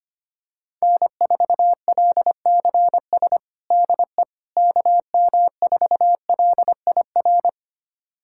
Morse code, heard as a keyed tone, about 25 wpm, 700 Hz. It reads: N4LCS DE KM4LIR
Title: USA Callsign DE Callsign 25wpm